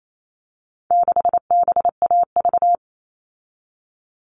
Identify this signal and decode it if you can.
Morse code, heard as a keyed tone, about 28 wpm, 700 Hz. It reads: 6BAV
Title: All Cut Numbers - 4-Digit 28wpm